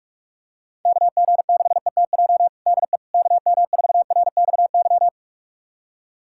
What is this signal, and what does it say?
Morse code, heard as a keyed tone, about 45 wpm, 700 Hz. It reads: KG6ETJ DE KM4RXY
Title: USA Callsign DE Callsign 45wpm